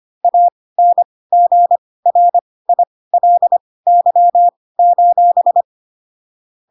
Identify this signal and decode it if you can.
Morse code, heard as a keyed tone, about 25 wpm, spaced as for 18 wpm, 700 Hz. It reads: ANGRILY:
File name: Pinocchio ch3 12 25 ik1hge